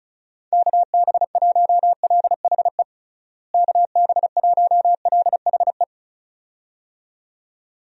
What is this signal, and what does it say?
Morse code, heard as a keyed tone, about 34 wpm, 700 Hz. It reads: KB1LHE KB1LHE
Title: USA Call Signs - 6 Characters - Repeated 2x - 35wpm